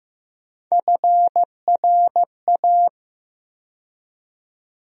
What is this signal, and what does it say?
Morse code, heard as a keyed tone, about 15 wpm, 700 Hz. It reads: FRA
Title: USA Call Sign Postfixes - 3 Letters 15wpm